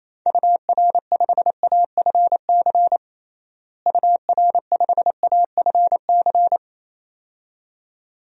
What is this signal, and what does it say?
Morse code, heard as a keyed tone, about 28 wpm, 700 Hz. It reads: UR5AFC UR5AFC
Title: DXCC-100 Call Signs - 6 Characters - Repeated 2x - 28wpm